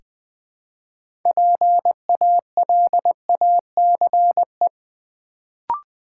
Morse code, heard as a keyed tone, about 20 wpm, 700 Hz. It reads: PALACE
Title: Build A Word - 3 Letters At A Time - 6-Letter Words - 20wpm